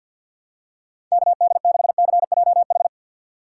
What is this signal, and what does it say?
Morse code, heard as a keyed tone, about 46 wpm, 700 Hz. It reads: KD6CJH
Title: USA Call Signs - 2x Character Spacing - 50wpm